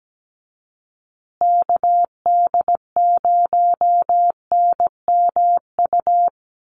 Morse code, heard as a keyed tone, about 17 wpm, 700 Hz. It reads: KD0NMU
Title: USA Call Signs Repeated 2x 17wpm